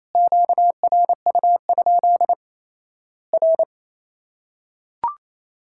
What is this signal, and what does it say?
Morse code, heard as a keyed tone, about 27 wpm, 700 Hz. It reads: QRU? R
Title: Worked All Europe DX Contest - QTC Exchange - 28wpm